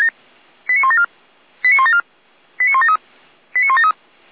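Five-tone selective call (ZVEI), heard as quick runs of five tones, tones of about 70 ms. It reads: 89174 89174 89173 89173
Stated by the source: ZVEI five-tone call received on SDR, feuerwehr-orlamuende fire brigade alerting.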